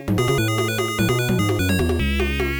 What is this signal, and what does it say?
There is no code.